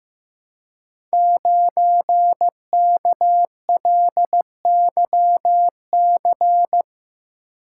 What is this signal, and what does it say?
Morse code, heard as a keyed tone, about 15 wpm, 700 Hz. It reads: 9KLYC
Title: Code Groups - 5 Letter:Numbers - 15wpm